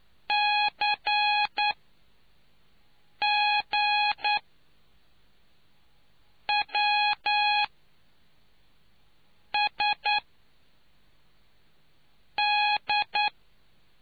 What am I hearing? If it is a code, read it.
Morse code, heard as a keyed tone, about 9 wpm, spaced as for 4 wpm, 830 Hz. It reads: CGWSD